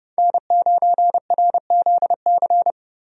Morse code, heard as a keyed tone, about 30 wpm, 700 Hz. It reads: N9RZC
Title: USA Call Signs - 5 Characters - 30wpm